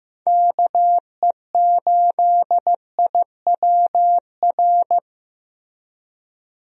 Morse code, heard as a keyed tone, about 15 wpm, 700 Hz. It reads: KE8IWR
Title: USA Call Signs - 6 Characters - Repeated 2x - 15wpm